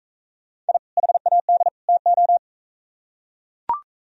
Morse code, heard as a keyed tone, about 44 wpm, 700 Hz. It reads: I HAD TO